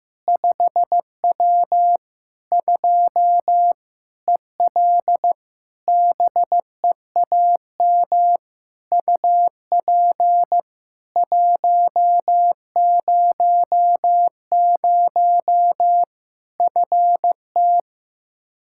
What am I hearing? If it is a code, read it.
Morse code, heard as a keyed tone, about 15 wpm, 700 Hz. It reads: 5W 2 EL BEAM UP 100 FT
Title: K1USN SST QSO Practice 15wpm